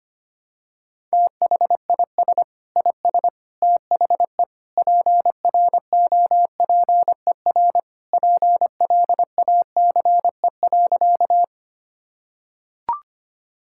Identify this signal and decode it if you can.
Morse code, heard as a keyed tone, about 25 wpm, 700 Hz. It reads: THIS IS THE PROPER PLACE.
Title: Sentences from Top 1000 Words - Review Part 5 - 25wpm